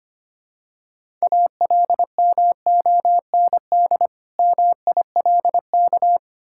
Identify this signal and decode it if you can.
Morse code, heard as a keyed tone, about 25 wpm, 700 Hz. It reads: ALMOND MILK
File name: practice / 25wpm / Difficult Words with Hints - 25wpm